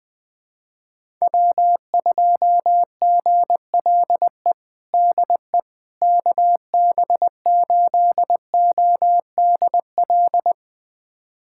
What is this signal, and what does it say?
Morse code, heard as a keyed tone, about 20 wpm, 700 Hz. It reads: W2GLE DE KB8ODL
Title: USA Callsign DE Callsign 20wpm